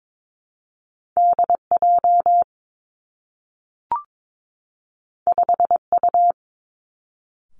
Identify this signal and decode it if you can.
Morse code, heard as a keyed tone, about 22 wpm, 700 Hz. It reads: D J 5 U
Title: Two Challenging Letter-Number Combinations 22wpm